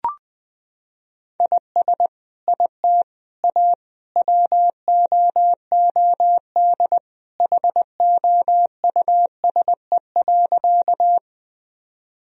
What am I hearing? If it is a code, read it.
Morse code, heard as a keyed tone, about 20 wpm, 700 Hz. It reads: IS IT A WOOD HOUSE.
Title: Sentences from Top 1000 Words - Review Part 2 - 20wpm